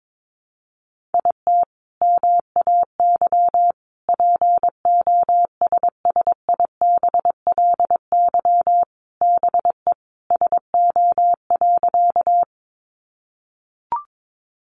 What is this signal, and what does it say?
Morse code, heard as a keyed tone, about 22 wpm, 700 Hz. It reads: IT MAY POSSIBLY BE SO.
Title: Sentences from Top 1300 Words 22wpm